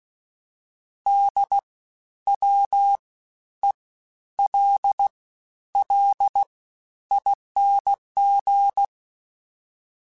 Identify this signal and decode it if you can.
Morse code, heard as a keyed tone, about 16 wpm, 800 Hz. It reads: D W E L L ING
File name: Special INGs-mp3s / Special INGs-mp3s / Fast ing @15